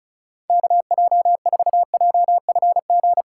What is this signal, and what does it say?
Morse code, heard as a keyed tone, about 34 wpm, 700 Hz. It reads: KJ4JFG
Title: USA Call Signs - 6x Character Spacing - 35wpm